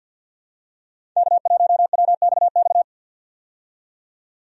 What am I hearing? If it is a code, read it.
Morse code, heard as a keyed tone, about 49 wpm, 700 Hz. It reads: K1WXX
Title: USA Call Signs - 5 Characters - 50wpm